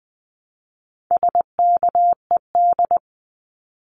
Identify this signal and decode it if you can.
Morse code, heard as a keyed tone, about 20 wpm, 700 Hz. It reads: SKED
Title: Abbreviations Repeated 3x - 20wpm